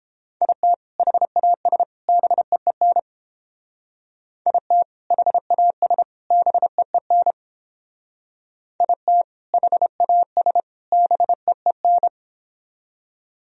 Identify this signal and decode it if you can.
Morse code, heard as a keyed tone, about 31 wpm, 700 Hz. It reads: IT HAS BEEN IT HAS BEEN IT HAS BEEN